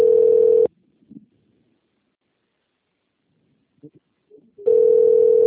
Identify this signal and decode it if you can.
Ringback tone (North America), 440 + 480 Hz.